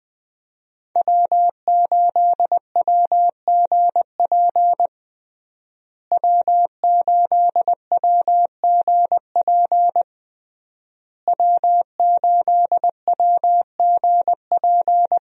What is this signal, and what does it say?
Morse code, heard as a keyed tone, about 20 wpm, 700 Hz. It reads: W8WGP W8WGP W8WGP
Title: USA Call Signs - 5 Characters - Repeated 3x - 20wpm